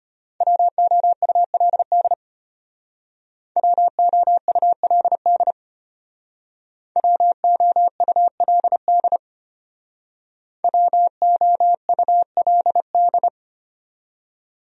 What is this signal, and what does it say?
Morse code, heard as a keyed tone, about 32 wpm, 700 Hz. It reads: WOULD WOULD WOULD WOULD